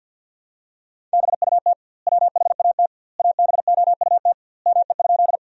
Morse code, heard as a keyed tone, about 46 wpm, 700 Hz. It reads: BUT WHAT ABOUT ME?
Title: 2x Word-Spacing - Sentences from Top 200 Words - 50wpm